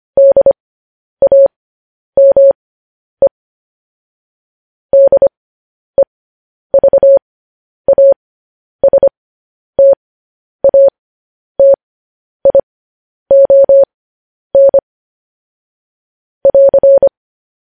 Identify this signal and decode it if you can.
Morse code, heard as a keyed tone, about 25 wpm, spaced as for 10 wpm, 550 Hz. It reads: DAME DEVASTATION <AR>